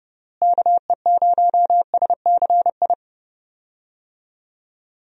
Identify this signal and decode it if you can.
Morse code, heard as a keyed tone, about 30 wpm, 700 Hz. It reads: KE0SCI
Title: USA Call Signs - 6 Characters - Repeated 2x - 30wpm